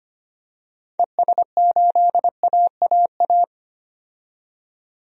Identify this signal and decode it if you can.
Morse code, heard as a keyed tone, about 25 wpm, 700 Hz. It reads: ES8AAA
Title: DXCC-100 Call Signs - 6 Characters - 25wpm